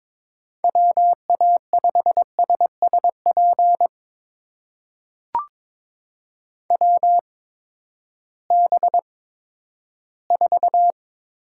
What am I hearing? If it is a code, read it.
Morse code, heard as a keyed tone, about 22 wpm, spaced as for 13 wpm, 700 Hz. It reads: WA5SSP WB4
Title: USA Call Signs - 6 Characters - 8x Character Spacing - 22wpm